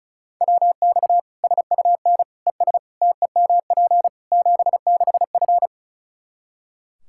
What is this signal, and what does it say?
Morse code, heard as a keyed tone, about 34 wpm, 700 Hz. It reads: WX SUN ES TEMP 76F
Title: K1USN SST QSO Practice 35wpm